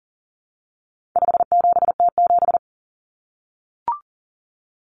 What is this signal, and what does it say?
Morse code, heard as a keyed tone, about 40 wpm, 700 Hz. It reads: 57T7
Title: Common Cut Numbers - 4-Digit 40wpm